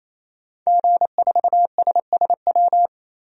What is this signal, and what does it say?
Morse code, heard as a keyed tone, about 28 wpm, 700 Hz. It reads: G4SSW
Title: DXCC-100 Call Signs - 5 Characters - Repeated 3x - 28wpm